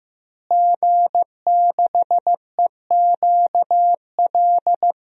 Morse code, heard as a keyed tone, about 15 wpm, 700 Hz. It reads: G6EQL
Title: DXCC-100 Call Signs - 5 Characters - Repeated 3x - 15wpm